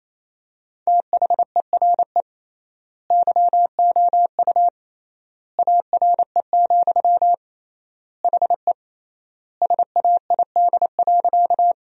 Morse code, heard as a keyed tone, about 28 wpm, 700 Hz. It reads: THERE YOU ARE, HE SAID.
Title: 3x Word-Spacing - Sentences from Top 100 Words - 28wpm